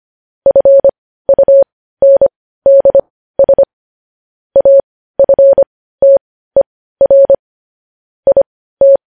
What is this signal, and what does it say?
Morse code, heard as a keyed tone, about 25 wpm, spaced as for 15 wpm, 550 Hz. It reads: FUNDS AFTER IT
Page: news headlines in morse code at 15 wpm